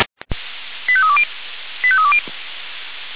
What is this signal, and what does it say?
Five-tone selective call (ZVEI), heard as quick runs of five tones, tones of about 70 ms. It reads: 85320 85320